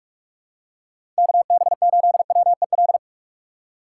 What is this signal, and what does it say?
Morse code, heard as a keyed tone, about 44 wpm, 700 Hz. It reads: KB8WEL
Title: USA Call Signs - 6 Characters - 6x Character Spacing - 45wpm